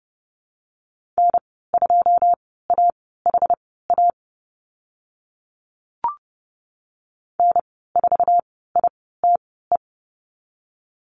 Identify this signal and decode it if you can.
Morse code, heard as a keyed tone, about 30 wpm, spaced as for 17 wpm, 700 Hz. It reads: N2AHA N4ITE
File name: Lesson 15g - Number 5 - Call Signs 30wpm